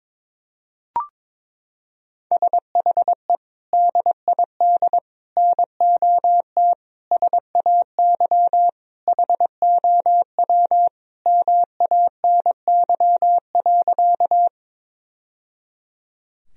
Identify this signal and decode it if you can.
Morse code, heard as a keyed tone, about 22 wpm, 700 Hz. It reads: SHE DID NOT SAY HOW MANY.
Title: Sentences from Top 1000 Words - Review Part 1 - 22wpm